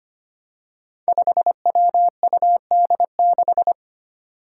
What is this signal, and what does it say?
Morse code, heard as a keyed tone, about 25 wpm, 700 Hz. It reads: 5WUD6
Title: Five Challenging Letter-Number Combinations 25wpm